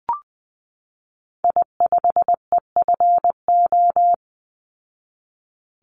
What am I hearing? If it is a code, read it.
Morse code, heard as a keyed tone, about 20 wpm, 700 Hz. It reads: I5EFO
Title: Call Signs Repeated 3x - 20wpm